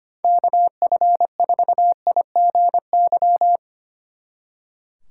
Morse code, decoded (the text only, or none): KF4IGY